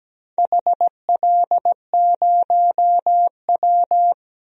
Morse code, heard as a keyed tone, about 17 wpm, 700 Hz. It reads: HL0W
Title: DXCC-100 Call Signs - 4 Characters - 17wpm